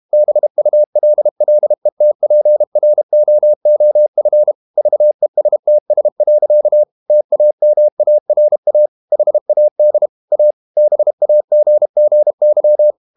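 Morse code, heard as a keyed tone, about 32 wpm, 600 Hz. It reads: BULLETPROOF VESTS. TAMARA HAD A BAGGY
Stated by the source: Morse code practice at 32 wpm from CW med Gnister, reading Ken Follett's novel Never in English.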